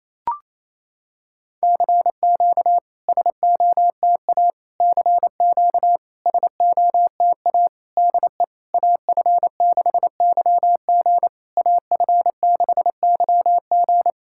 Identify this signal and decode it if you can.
Morse code, heard as a keyed tone, about 28 wpm, 700 Hz. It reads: CQ SOTA CQ SOTA DE AF6YG AF6YG
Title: SOTA QSO Practice 28wpm